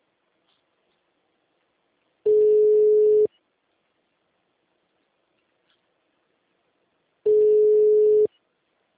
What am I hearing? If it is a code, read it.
Ringback tone (Europe, 425 Hz).